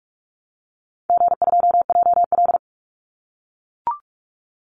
Morse code, heard as a keyed tone, about 44 wpm, 700 Hz. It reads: G2JL
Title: Call Signs - 45wpm